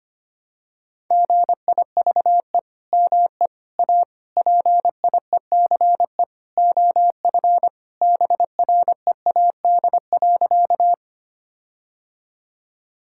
Morse code, heard as a keyed tone, about 25 wpm, 700 Hz. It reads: GIVE ME A PIECE OF BREAD.
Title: Sentences from Top 1600 Words 25wpm